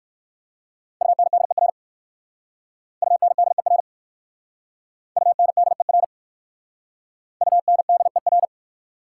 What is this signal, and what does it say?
Morse code, heard as a keyed tone, about 59 wpm, 690 Hz. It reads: UNDER UNDER UNDER UNDER